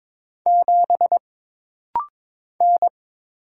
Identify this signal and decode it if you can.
Morse code, heard as a keyed tone, about 22 wpm, 700 Hz. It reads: 7 N
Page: This rapid-fire timing is geared to push your ICR (Instant Character Recognition) skills.